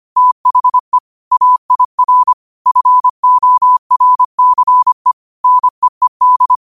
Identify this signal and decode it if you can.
Morse code, heard as a keyed tone, about 25 wpm, 1000 Hz. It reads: THE AIR FORCE NEED